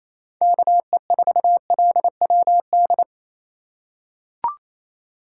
Morse code, heard as a keyed tone, about 27 wpm, 700 Hz. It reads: KE4LWD
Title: USA Call Signs - 6 Characters - 3x Character Spacing - 28wpm